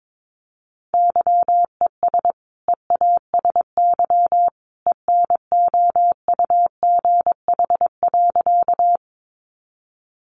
Morse code, heard as a keyed tone, about 22 wpm, 700 Hz. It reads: YES EASY ENOUGH.
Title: Sentences from Top 700 Words 22wpm